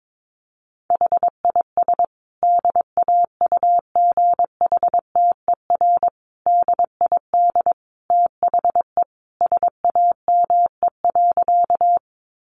Morse code, heard as a keyed tone, about 22 wpm, 700 Hz. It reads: HIS DAUGHTER DID THE SAME.
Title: Sentences from Top 1500 Words 22wpm